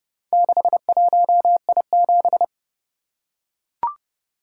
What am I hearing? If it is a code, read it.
Morse code, heard as a keyed tone, about 30 wpm, 700 Hz. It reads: 61I7